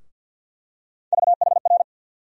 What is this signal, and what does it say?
Morse code, heard as a keyed tone, about 47 wpm, 700 Hz. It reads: VHR
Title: USA Call Sign Postfixes - 3 Letters 50wpm